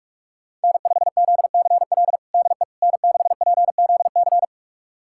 Morse code, heard as a keyed tone, about 46 wpm, 700 Hz. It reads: N5ZCL DE N6PZC
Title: USA Callsign DE Callsign 45wpm